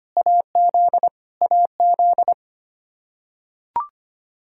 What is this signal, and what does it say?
Morse code, heard as a keyed tone, about 25 wpm, 700 Hz. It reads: AZ AZ
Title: Practice QSO - Exchange 1 - Beginning - 25wpm